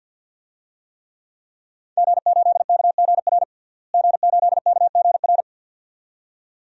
Morse code, heard as a keyed tone, about 52 wpm, 700 Hz. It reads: G8KGR G8KGR
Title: DXCC-100 Call Signs - 5 Characters - Repeated 2x - 50wpm